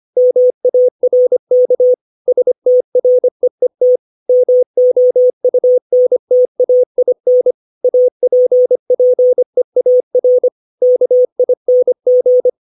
Morse code, heard as a keyed tone, about 25 wpm, 500 Hz. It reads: MARK STREET MOUNTAIN APPEAR KING
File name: 632 words / 632 Words 25WPM